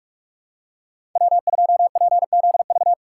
Morse code, heard as a keyed tone, about 42 wpm, 700 Hz. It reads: W2PZV
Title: USA Call Signs - 5 Characters - 3x Character Spacing - 45wpm